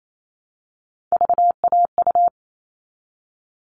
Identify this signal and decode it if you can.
Morse code, heard as a keyed tone, about 28 wpm, 700 Hz. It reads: VAU
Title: All Cut Numbers - 3-Digit 28wpm